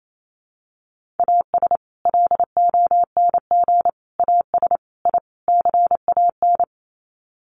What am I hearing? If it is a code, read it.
Morse code, heard as a keyed tone, about 28 wpm, 700 Hz. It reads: AS LONG AS I CAN
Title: Sets of 5 Words Encore2 28wpm